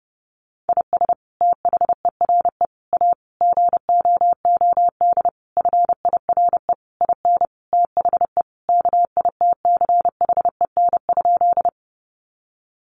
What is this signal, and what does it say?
Morse code, heard as a keyed tone, about 30 wpm, 700 Hz. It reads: IS THERE A GOOD FIRE IN THE KITCHEN?
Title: Sentences from Top 1400 Words 30wpm